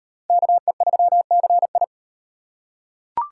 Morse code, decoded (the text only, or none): KE3CI